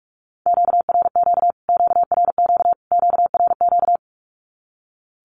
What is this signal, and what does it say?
Morse code, heard as a keyed tone, about 44 wpm, 700 Hz. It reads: QRQ QRQ QRQ
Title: Q-Codes Repeated 3x 45wpm